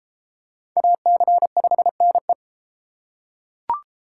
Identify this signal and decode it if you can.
Morse code, heard as a keyed tone, about 29 wpm, 700 Hz. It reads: AC5NE